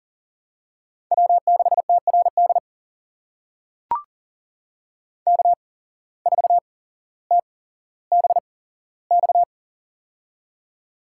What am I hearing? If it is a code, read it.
Morse code, heard as a keyed tone, about 38 wpm, spaced as for 17 wpm, 700 Hz. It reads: W6TRD K4TBX
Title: USA Call Signs - 5 Characters - 8x Character Spacing - 40wpm